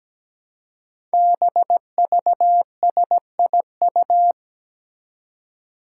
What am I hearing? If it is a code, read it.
Morse code, heard as a keyed tone, about 17 wpm, 700 Hz. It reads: BVSIU
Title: Five Challenging Letter-Number Combinations 17wpm